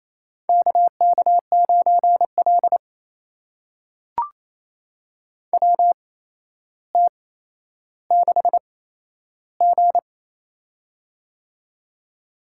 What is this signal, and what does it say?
Morse code, heard as a keyed tone, about 28 wpm, spaced as for 12 wpm, 700 Hz. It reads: KK9L WT6G
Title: USA Call Signs - 4 Characters - 8x Character Spacing - 28wpm